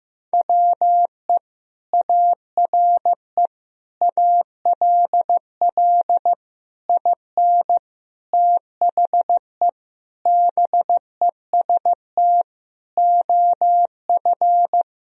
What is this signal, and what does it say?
Morse code, heard as a keyed tone, about 15 wpm, 700 Hz. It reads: WE ARE ALL IN THE BEST OF
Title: Sentences from Top 1000 Words - Review Part 3 - 15wpm